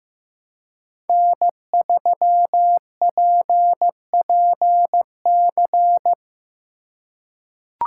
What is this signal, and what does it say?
Morse code, heard as a keyed tone, about 16 wpm, 700 Hz. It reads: N3PPC
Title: USA Call Signs - 5 Characters - 4x Character Spacing - 15wpm